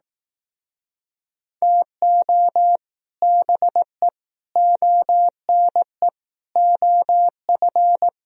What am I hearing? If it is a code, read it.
Morse code, heard as a keyed tone, about 19 wpm, 700 Hz. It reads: TO BE ONE OF
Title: Speed-Racing - Sets of 4 Words 15wpm